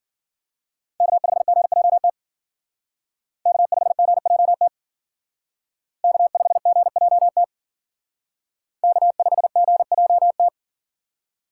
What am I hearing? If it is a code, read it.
Morse code, heard as a keyed tone, about 51 wpm, 700 Hz. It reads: K5GJT K5GJT K5GJT K5GJT